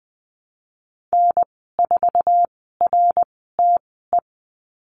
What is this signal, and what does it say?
Morse code, heard as a keyed tone, about 20 wpm, spaced as for 14 wpm, 700 Hz. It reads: N4RTE
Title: 055 Lesson 10g - Letter r - Call Signs 20wpm